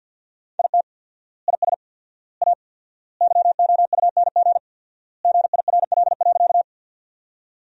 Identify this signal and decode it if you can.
Morse code, heard as a keyed tone, about 50 wpm, 700 Hz. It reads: IT IS A YOUNG GIRL.
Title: 4x Word-Spacing - Sentences from Top 400 Words - 50wpm